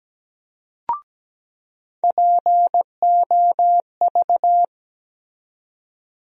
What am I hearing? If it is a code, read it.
Morse code, heard as a keyed tone, about 17 wpm, 700 Hz. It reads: POV
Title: USA Call Sign Postfixes - 3 Letters 17wpm